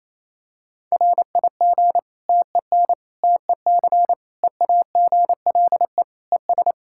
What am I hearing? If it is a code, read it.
Morse code, heard as a keyed tone, about 27 wpm, 700 Hz. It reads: RIG TEN TEC EAGLE ES
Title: Practice QSO - Exchange 2 - Info - 28wpm